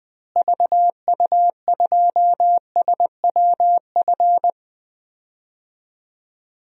Morse code, heard as a keyed tone, about 20 wpm, 700 Hz. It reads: VU2SWF
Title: DXCC-100 Call Signs - 6 Characters - Repeated 2x - 20wpm